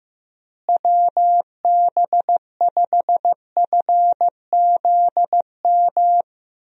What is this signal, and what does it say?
Morse code, heard as a keyed tone, about 15 wpm, 700 Hz. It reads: WB5FZM
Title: USA Call Signs - 6 Characters - Repeated 3x - 15wpm